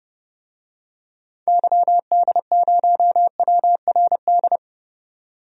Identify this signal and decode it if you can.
Morse code, heard as a keyed tone, about 30 wpm, 700 Hz. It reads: YD0WRD
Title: DXCC-100 Call Signs - 6 Characters - 30wpm